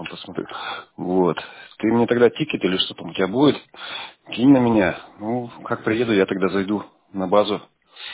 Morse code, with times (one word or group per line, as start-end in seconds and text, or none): none